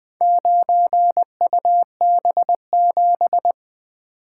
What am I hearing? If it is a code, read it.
Morse code, heard as a keyed tone, about 20 wpm, 700 Hz. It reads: 9UB7
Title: Four Letter-Number Combinations - 20wpm